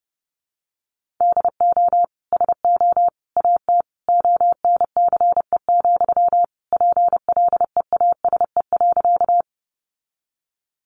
Morse code, heard as a keyed tone, about 30 wpm, 700 Hz. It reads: DO SO AT ONCE, PLEASE.
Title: Sentences from Top 1500 Words 30wpm